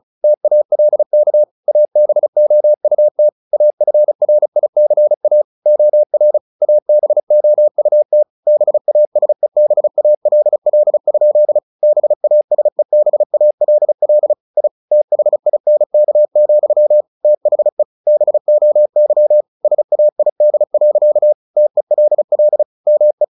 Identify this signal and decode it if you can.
Morse code, heard as a keyed tone, about 34 wpm, 600 Hz. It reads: TALK ABOUT AFRICA OR ABOUT BASEBALL? BASEBALL I THINK, THE BOY SAID. TELL ME
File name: the old man and the sea / the old man and the sea 1 35wpm